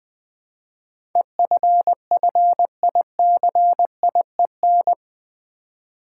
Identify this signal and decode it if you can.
Morse code, heard as a keyed tone, about 20 wpm, 700 Hz. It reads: EFFICIEN
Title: Build A Word - 4 Letters At A Time - 11-Letter Words - 20wpm